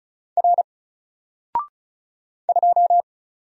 Morse code, heard as a keyed tone, about 33 wpm, 700 Hz. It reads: R 2
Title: Single Letter-Number 35wpm